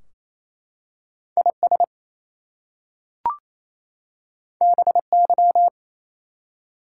Morse code, heard as a keyed tone, about 27 wpm, 700 Hz. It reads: IS BY
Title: Two Letter Words - 28wpm